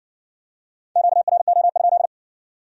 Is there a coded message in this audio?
Yes: Morse code from 1.0 to 2.1 s, heard as a keyed tone, about 57 wpm, 700 Hz. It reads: QRO?